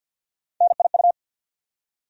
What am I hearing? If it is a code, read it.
Morse code, heard as a keyed tone, about 52 wpm, 700 Hz. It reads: NIU
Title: USA Call Sign Postfixes - 3 Letters 50wpm